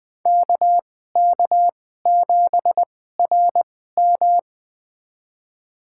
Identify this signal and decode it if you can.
Morse code, heard as a keyed tone, about 20 wpm, spaced as for 14 wpm, 700 Hz. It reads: KK7RM
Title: USA Call Signs - 5 Characters - 2x Character Spacing - 20wpm